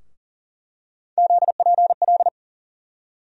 Morse code, heard as a keyed tone, about 40 wpm, 700 Hz. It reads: ZPL
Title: USA Call Sign Postfixes - 3 Letters 40wpm